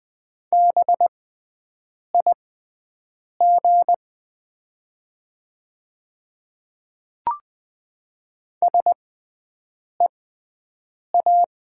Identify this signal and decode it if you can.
Morse code, heard as a keyed tone, about 21 wpm, spaced as for 7 wpm, 700 Hz. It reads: BIG SEA
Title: 6x Character Spacing - Top 300 Words - 20wpm